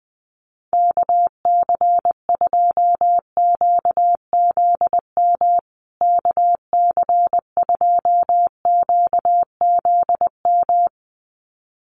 Morse code, heard as a keyed tone, about 20 wpm, 700 Hz. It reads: KC2QZM KC2QZM
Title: USA Call Signs Repeated 2x 20wpm